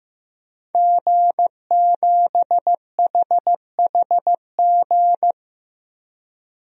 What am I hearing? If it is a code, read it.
Morse code, heard as a keyed tone, about 16 wpm, 700 Hz. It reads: G7HHG
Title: DXCC-100 Call Signs - 5 Characters - 15wpm